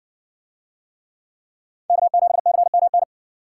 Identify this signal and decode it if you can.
Morse code, heard as a keyed tone, about 57 wpm, 700 Hz. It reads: K7CMN